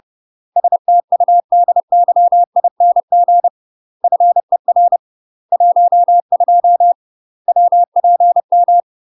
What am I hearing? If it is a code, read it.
Morse code, heard as a keyed tone, about 30 wpm, 700 Hz. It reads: STUDYING FER 12 WPM
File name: UK MASTER QSOs TRAINING / 30WPM / UK MASTER QSOs 0033